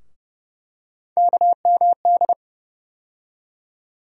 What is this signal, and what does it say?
Morse code, heard as a keyed tone, about 30 wpm, 700 Hz. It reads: KMD
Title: USA Call Sign Postfixes - 3 Letters 30wpm